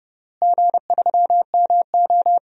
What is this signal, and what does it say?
Morse code, heard as a keyed tone, about 30 wpm, 700 Hz. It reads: G3MO
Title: DXCC-100 Call Signs - 4 Characters - Repeated 3x - 30wpm